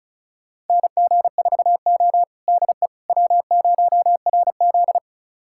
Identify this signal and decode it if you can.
Morse code, heard as a keyed tone, about 36 wpm, 700 Hz. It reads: NG4O DE W0RZ
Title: USA Callsign DE Callsign - 4 Characters - 35wpm